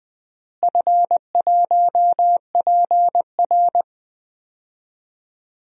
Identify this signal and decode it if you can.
Morse code, heard as a keyed tone, about 21 wpm, 700 Hz. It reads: F1PR
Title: DXCC-100 Call Signs - 4 Characters - Repeated 3x - 20wpm